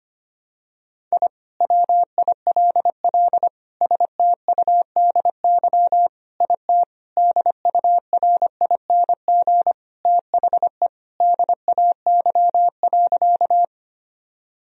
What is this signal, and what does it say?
Morse code, heard as a keyed tone, about 25 wpm, 700 Hz. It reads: I WILL STUDY IT DURING THE DAY.
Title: Sentences from Top 1000 Words - Review Part 2 - 25wpm